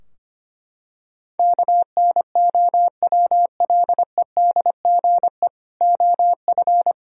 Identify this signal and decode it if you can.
Morse code, heard as a keyed tone, about 25 wpm, 700 Hz. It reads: KNOWLEDGE OF
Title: Sentences from Top 1000 Words - Review Part 3 - 25wpm